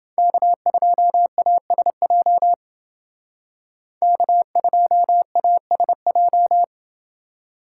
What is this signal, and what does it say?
Morse code, heard as a keyed tone, about 29 wpm, 700 Hz. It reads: K2ASJ K2ASJ